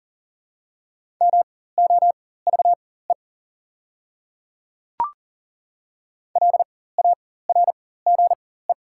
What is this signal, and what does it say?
Morse code, heard as a keyed tone, about 40 wpm, spaced as for 19 wpm, 700 Hz. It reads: MOVE LARGE
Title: 4x Character Spacing - Top 300 Words - 40wpm